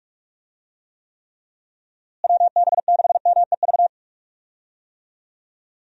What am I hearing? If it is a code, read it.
Morse code, heard as a keyed tone, about 46 wpm, 700 Hz. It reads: WB6MEV
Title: USA Call Signs - 6 Characters - Repeated 3x - 45wpm